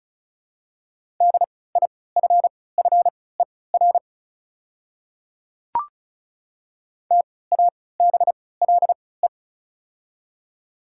Morse code, heard as a keyed tone, about 34 wpm, spaced as for 20 wpm, 690 Hz. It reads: DIFFER TABLE
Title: 3x Character Spacing - Top 500 Words - 35wpm